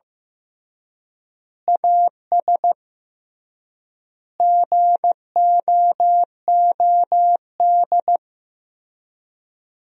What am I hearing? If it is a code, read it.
Morse code, heard as a keyed tone, about 15 wpm, 700 Hz. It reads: AS GOOD
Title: 3x Word-Spacing - Sentences from Top 200 Words - 15wpm